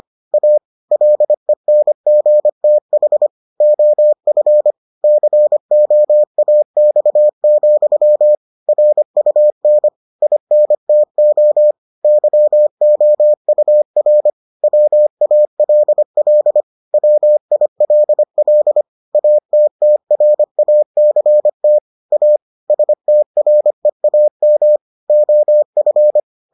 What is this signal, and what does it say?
Morse code, heard as a keyed tone, about 25 wpm, 600 Hz. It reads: A LENGTH OF COAX, RUN INTO YOUR WALL WILL ATTRACT A STREAM OF